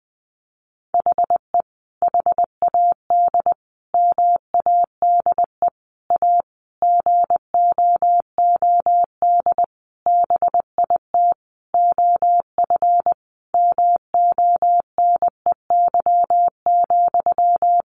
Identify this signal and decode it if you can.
Morse code, heard as a keyed tone, about 20 wpm, 700 Hz. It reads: HE HAD MADE A GOOD BIT OF MONEY,
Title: Sentences from Top 900 Words 20wpm